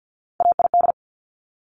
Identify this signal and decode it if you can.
Morse code, heard as a keyed tone, about 50 wpm, 690 Hz. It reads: AID
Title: Top Three Letter Combinations in Words 50wpm